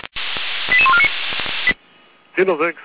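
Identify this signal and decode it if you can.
Five-tone selective call (ZVEI), heard as quick runs of five tones, tones of about 70 ms. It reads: 88149
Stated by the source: Five-tone ZVEI selective call 88159 received on an SDR, recorded by Feuerwehr Orlamünde.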